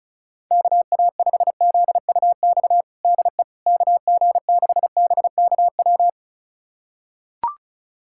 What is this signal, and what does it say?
Morse code, heard as a keyed tone, about 36 wpm, 700 Hz. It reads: KA5ZUX DE KG6BKW
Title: USA Callsign DE Callsign 35wpm